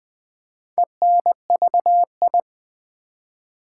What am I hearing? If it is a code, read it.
Morse code, heard as a keyed tone, about 20 wpm, 700 Hz. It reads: ENVI
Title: Build A Word - 4 Letters At A Time - 11-Letter Words - 20wpm